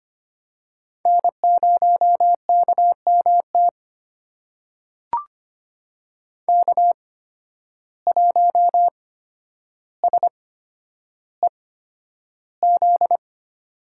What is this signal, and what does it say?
Morse code, heard as a keyed tone, about 25 wpm, spaced as for 11 wpm, 700 Hz. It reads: N0KMT K1SEZ
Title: USA Call Signs - 5 Characters - 8x Character Spacing - 25wpm